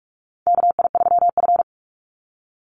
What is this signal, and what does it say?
Morse code, heard as a keyed tone, about 46 wpm, 700 Hz. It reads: KI3F
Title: Call Signs - 45wpm